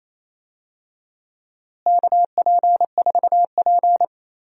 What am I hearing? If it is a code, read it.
Morse code, heard as a keyed tone, about 28 wpm, 700 Hz. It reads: KP4P
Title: DXCC-100 Call Signs - 4 Characters - Repeated 3x - 28wpm